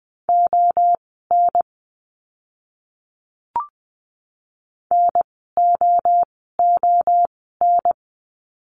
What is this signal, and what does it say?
Morse code, heard as a keyed tone, about 20 wpm, spaced as for 14 wpm, 700 Hz. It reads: ON NOON
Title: 030 Lesson 6f - Letter i - Words 20wpm